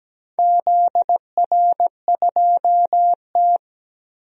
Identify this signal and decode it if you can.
Morse code, heard as a keyed tone, about 17 wpm, 700 Hz. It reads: ZR2T
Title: DXCC-100 Call Signs - 4 Characters - Repeated 3x - 17wpm